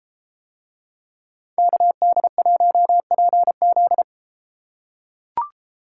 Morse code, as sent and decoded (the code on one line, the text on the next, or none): -.- -.. .---- .--. --..
KD1PZ